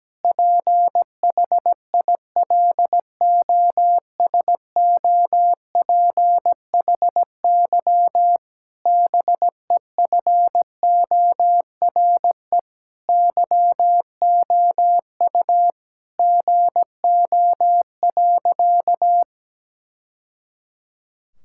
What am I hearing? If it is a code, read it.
Morse code, heard as a keyed tone, about 17 wpm, 700 Hz. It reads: PHILOSOPHY BEFORE YOU GO.
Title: Sentences from Top 1000 Words - Review Part 6 - 17wpm